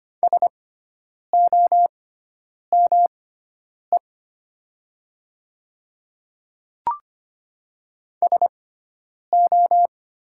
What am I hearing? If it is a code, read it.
Morse code, heard as a keyed tone, about 25 wpm, spaced as for 9 wpm, 700 Hz. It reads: SOME SO
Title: 6x Character Spacing - Top 100 Words - 25wpm